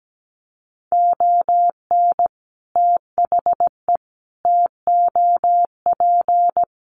Morse code, heard as a keyed tone, about 17 wpm, 700 Hz. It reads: ON THE TOP
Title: Sets of 5 Words Encore 17wpm